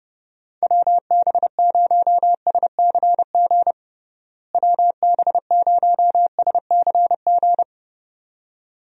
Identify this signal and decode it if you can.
Morse code, heard as a keyed tone, about 30 wpm, 700 Hz. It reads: WB0SCG WB0SCG
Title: USA Call Signs - 6 Characters - Repeated 2x - 30wpm